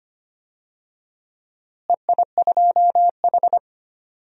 Morse code, heard as a keyed tone, about 25 wpm, 700 Hz. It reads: EI2H
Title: DXCC-100 Call Signs - 4 Characters - Repeated 3x - 25wpm